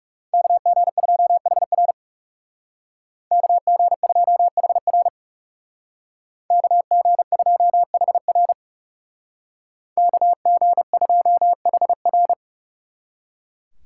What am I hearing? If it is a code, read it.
Morse code, heard as a keyed tone, about 35 wpm, 700 Hz. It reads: KG2HR KG2HR KG2HR KG2HR
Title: Speed Racing - USA 2x2 Call Signs 30wpm